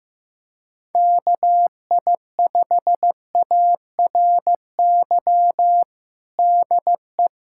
Morse code, heard as a keyed tone, about 15 wpm, 700 Hz. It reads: KI5ARY DE
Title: USA Callsign DE Callsign 15wpm